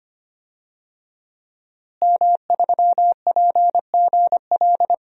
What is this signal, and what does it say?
Morse code, heard as a keyed tone, about 25 wpm, 700 Hz. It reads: M3PGL
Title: DXCC-100 Call Signs - 5 Characters - Repeated 2x - 25wpm